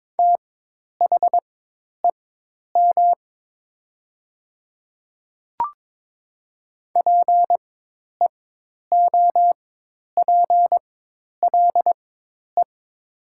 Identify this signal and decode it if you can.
Morse code, heard as a keyed tone, about 22 wpm, spaced as for 10 wpm, 700 Hz. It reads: THEM PEOPLE
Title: 4x Character Spacing - Top 300 Words - 22wpm